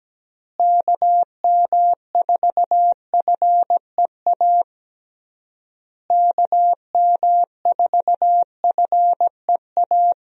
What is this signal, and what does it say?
Morse code, heard as a keyed tone, about 17 wpm, 700 Hz. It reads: KM4FEA KM4FEA
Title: USA Call Signs - 6 Characters - Repeated 3x - 17wpm